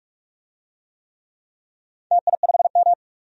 Morse code, heard as a keyed tone, about 47 wpm, 700 Hz. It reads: TI5M
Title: DXCC-100 Call Signs - 4 Characters - 45wpm